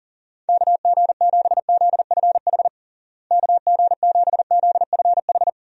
Morse code, heard as a keyed tone, about 40 wpm, 700 Hz. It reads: KG7ZFH KG7ZFH
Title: USA Call Signs - 6 Characters - Repeated 2x - 40wpm